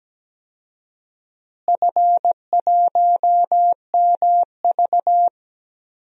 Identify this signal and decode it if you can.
Morse code, heard as a keyed tone, about 17 wpm, 700 Hz. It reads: F1MV
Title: DXCC-100 Call Signs - 4 Characters - Repeated 2x - 17wpm